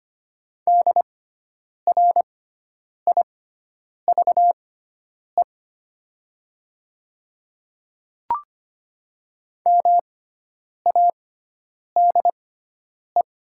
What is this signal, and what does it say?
Morse code, heard as a keyed tone, about 25 wpm, spaced as for 9 wpm, 700 Hz. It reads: DRIVE MADE
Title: 6x Character Spacing - Top 500 Words - 25wpm